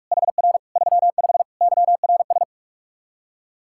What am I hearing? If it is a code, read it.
Morse code, heard as a keyed tone, about 46 wpm, 700 Hz. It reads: HR 35 YRS
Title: K1USN SST QSO Practice 45wpm